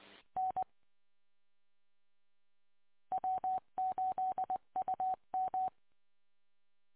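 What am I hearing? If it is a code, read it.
Morse code, heard as a keyed tone, about 20 wpm, 750 Hz. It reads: N W8UM
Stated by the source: Audio archive from a 145.230 MHz station.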